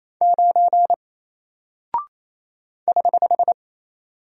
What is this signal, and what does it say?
Morse code, heard as a keyed tone, about 27 wpm, 700 Hz. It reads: 9 <HH>
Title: Single Character (Letter:Number:Punctuation:Prosign) 28wpm